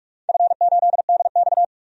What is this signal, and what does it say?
Morse code, heard as a keyed tone, about 47 wpm, 700 Hz. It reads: F8DX
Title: DXCC-100 Call Signs - 4 Characters - 45wpm